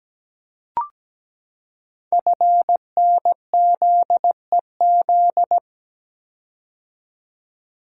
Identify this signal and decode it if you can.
Morse code, heard as a keyed tone, about 17 wpm, 700 Hz. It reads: FNZEZ
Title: Code Groups - 5 Letters - 17wpm